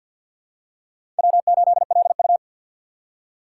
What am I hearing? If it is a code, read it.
Morse code, heard as a keyed tone, about 47 wpm, 700 Hz. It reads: W8LU
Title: USA Call Signs - 4 Characters - 3x Character Spacing - 50wpm